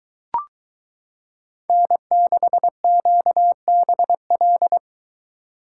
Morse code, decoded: N6QBL

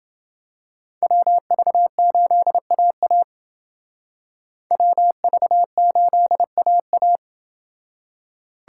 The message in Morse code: WV8AA WV8AA

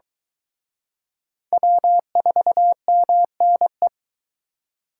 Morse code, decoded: W4MNE

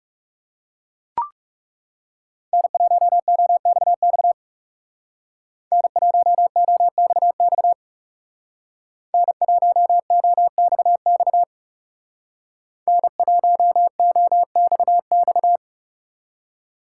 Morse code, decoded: N1OXX N1OXX N1OXX N1OXX

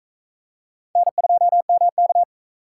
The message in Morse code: N2MK

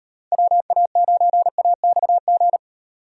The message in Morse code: WA9AXG

